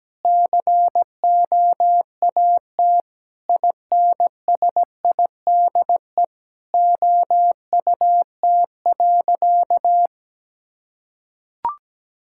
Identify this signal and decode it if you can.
Morse code, heard as a keyed tone, about 17 wpm, 700 Hz. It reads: COAT INSIDE OUT.
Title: Sentences from Top 1600 Words 17wpm